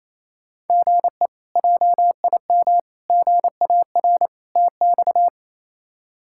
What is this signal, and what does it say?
Morse code, heard as a keyed tone, about 28 wpm, 700 Hz. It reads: GE JIM GAR TX
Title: K1USN SST QSO Practice - 28wpm